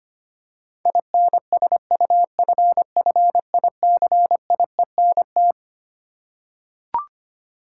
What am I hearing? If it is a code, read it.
Morse code, heard as a keyed tone, about 25 wpm, 700 Hz. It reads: INSUFFICIENT
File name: Twelve Letter Words - 25wpm